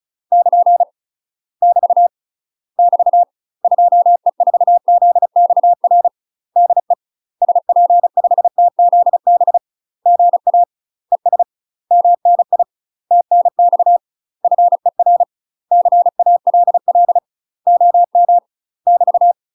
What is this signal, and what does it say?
Morse code, heard as a keyed tone, about 34 wpm, 700 Hz. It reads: <KN> <BT> <BT> 2E4ZXR DE SP5TZB GA ES MNI TNX FER CALL OM <BT>
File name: UK MASTER QSOs TRAINING / 35WPM / UK MASTER QSOs 0001